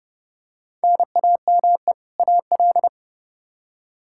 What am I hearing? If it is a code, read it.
Morse code, heard as a keyed tone, about 29 wpm, 700 Hz. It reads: NAME AL
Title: Practice QSO - Exchange 1 - Beginning - 30wpm